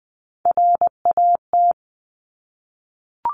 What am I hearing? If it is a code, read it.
Morse code, heard as a keyed tone, about 20 wpm, 700 Hz. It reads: RAT
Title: Top Three Letter Combinations in Words 20wpm